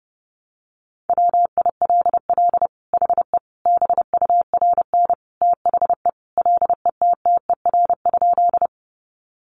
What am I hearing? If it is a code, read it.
Morse code, heard as a keyed tone, about 30 wpm, 700 Hz. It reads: WILL HE BURN THE LETTER?
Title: Sentences from Top 1300 Words 30wpm